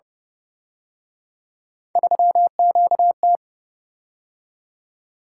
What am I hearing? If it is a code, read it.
Morse code, heard as a keyed tone, about 32 wpm, 700 Hz. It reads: 3QT